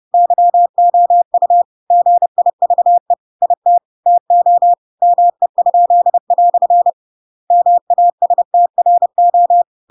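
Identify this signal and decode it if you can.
Morse code, heard as a keyed tone, about 30 wpm, 700 Hz. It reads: YOU GIVE IT TO ME?" MASTRO
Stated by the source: Chapter 2 of The Adventures of Pinocchio read as a book in Morse code (CW) at 30 wpm, by ik1hge.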